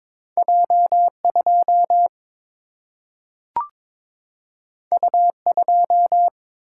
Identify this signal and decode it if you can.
Morse code, heard as a keyed tone, about 22 wpm, 700 Hz. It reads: J2 U2
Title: Single-Letter Single-Number 22wpm